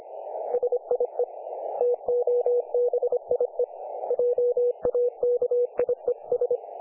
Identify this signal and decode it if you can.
Morse code, heard as a keyed tone, about 37 wpm, spaced as for 32 wpm, 500 Hz. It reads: SIE TOBIE JAKIES
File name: dlazaawansowanych / nagraniazpasma / websdr recording 3564.7kHz